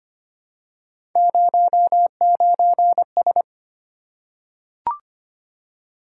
Morse code, heard as a keyed tone, about 25 wpm, 700 Hz. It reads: 09S
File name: Three Letter-Number Combinations - 25wpm